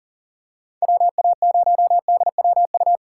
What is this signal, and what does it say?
Morse code, heard as a keyed tone, about 40 wpm, 700 Hz. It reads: WA0DWU
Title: USA Call Signs - 6 Characters - 6x Character Spacing - 40wpm